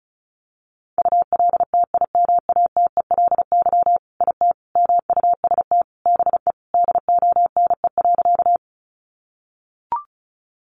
Morse code, heard as a keyed tone, about 34 wpm, 700 Hz. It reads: ULTIMATELY IT MUST BE DONE.
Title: Sentences from Top 1200 Words 35wpm